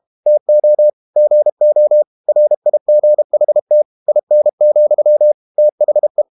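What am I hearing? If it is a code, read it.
Morse code, heard as a keyed tone, about 32 wpm, 600 Hz. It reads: TO GO RIGHT IN, THE
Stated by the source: Morse code practice at 32 wpm from CW med Gnister, reading Ken Follett's novel Never in English.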